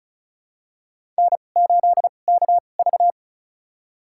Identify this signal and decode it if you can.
Morse code, heard as a keyed tone, about 36 wpm, spaced as for 25 wpm, 690 Hz. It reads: N8KV
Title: USA Call Signs - 4 Characters - 2x Character Spacing - 35wpm